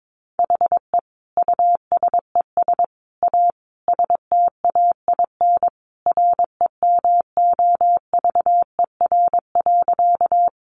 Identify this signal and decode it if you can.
Morse code, heard as a keyed tone, about 22 wpm, 700 Hz. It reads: HE USES A STAIN REMOVER.
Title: Additional Easy Phrases 22wpm